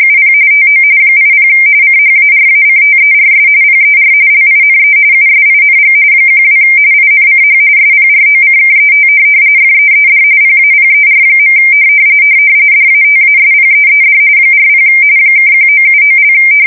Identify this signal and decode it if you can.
RTTY, about 45.45 baud. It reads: UP TO 80 CHARACTERS PER LINE USING 12 INCH SPROCKET FEED PAPER. DATASPEED 40 (1979)
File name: fsk170 / teletype-corp